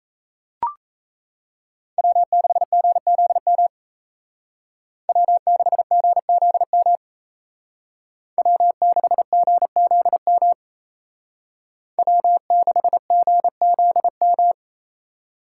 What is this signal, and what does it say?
Morse code, heard as a keyed tone, about 34 wpm, 700 Hz. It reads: W6GZM W6GZM W6GZM W6GZM
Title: Speed Racing - USA 1x3 Call Signs 28wpm